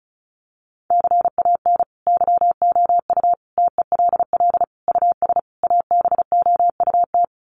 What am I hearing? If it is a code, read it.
Morse code, heard as a keyed tone, about 34 wpm, 700 Hz. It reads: CAN YOU TELL US ABOUT
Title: Sets of 5 Words Encore2 35wpm